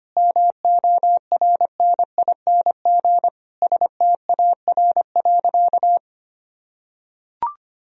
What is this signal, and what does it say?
Morse code, heard as a keyed tone, about 25 wpm, 700 Hz. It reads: MORNING STAR.
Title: Sentences from Top 1000 Words - Review Part 3 - 25wpm